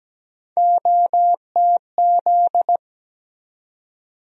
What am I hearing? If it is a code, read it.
Morse code, heard as a keyed tone, about 17 wpm, 700 Hz. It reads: OTZ
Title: USA Call Sign Postfixes - 3 Letters 17wpm